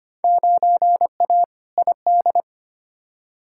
Morse code, heard as a keyed tone, about 25 wpm, 700 Hz. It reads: 9A ID
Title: Field Day Exchanges Encore - 25wpm